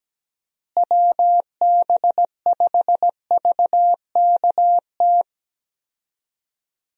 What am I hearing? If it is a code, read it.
Morse code, heard as a keyed tone, about 17 wpm, 700 Hz. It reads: WB5VKT
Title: USA Call Signs - 6 Characters - Repeated 3x - 17wpm